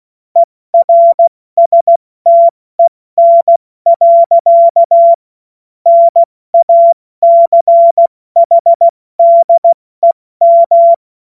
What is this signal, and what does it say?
Morse code, heard as a keyed tone, about 16 wpm, 680 Hz. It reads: ERSTEN. NACHDEM
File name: Morsekurs / Witze / Witz 15. August 2023 70 F80